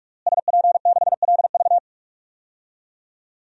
Morse code, heard as a keyed tone, about 46 wpm, 700 Hz. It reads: SP6LV
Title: DXCC-100 Call Signs - 5 Characters - Repeated 2x - 45wpm